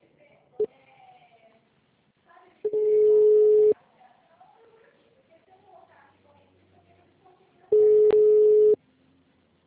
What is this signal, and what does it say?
Ringback tone (Europe, 425 Hz), with speech.
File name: out- -2013- - - .64711